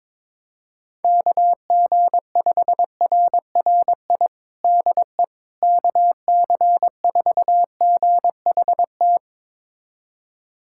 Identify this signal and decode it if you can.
Morse code, heard as a keyed tone, about 22 wpm, 700 Hz. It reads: KG5RRI DE KC4GHT
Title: USA Callsign DE Callsign - 6 Characters - 22wpm